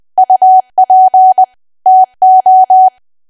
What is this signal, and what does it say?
Morse code, heard as a keyed tone, about 20 wpm, 750 Hz. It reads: UP TO
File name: Archive / 20 WPM / 20WPM